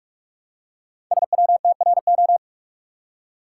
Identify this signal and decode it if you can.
Morse code, heard as a keyed tone, about 44 wpm, 700 Hz. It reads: SWTRO
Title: Code Groups - 5 Letters - 45wpm